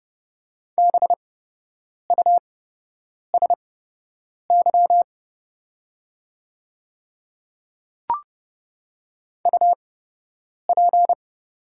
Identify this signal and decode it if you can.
Morse code, heard as a keyed tone, about 30 wpm, spaced as for 8 wpm, 690 Hz. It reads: BUSY UP
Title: 8x Character Spacing - Top 500 Words - 30wpm